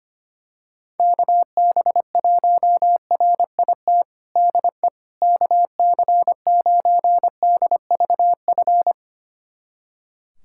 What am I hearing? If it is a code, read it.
Morse code, heard as a keyed tone, about 25 wpm, 700 Hz. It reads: KB1RIT DE KC9DVF
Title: USA Callsign DE Callsign - 6 Characters - 25wpm